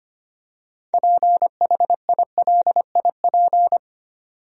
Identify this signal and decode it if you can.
Morse code, heard as a keyed tone, about 25 wpm, 700 Hz. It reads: PHILIP
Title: USA Common Names - 25wpm